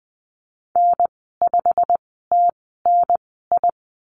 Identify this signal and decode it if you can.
Morse code, heard as a keyed tone, about 20 wpm, spaced as for 15 wpm, 700 Hz. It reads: N5TNI
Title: 147 Lesson 24g - Letter y - Call Signs 20wpm